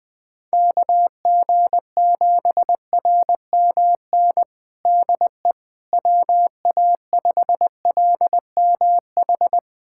Morse code, heard as a keyed tone, about 21 wpm, 700 Hz. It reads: KG7RMN DE WA5LMH
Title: USA Callsign DE Callsign - 6 Characters - 20wpm